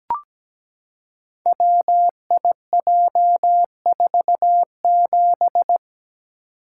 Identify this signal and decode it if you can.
Morse code, heard as a keyed tone, about 17 wpm, 700 Hz. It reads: WIJ47
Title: Code Groups - 5 Letter:Numbers - 17wpm